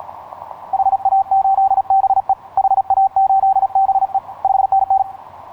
Morse code, heard as a keyed tone, about 37 wpm, 780 Hz. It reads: HA8BE HA8BE 5NN
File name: yj0ha ha8be etc 15m cw 10nov2010 0839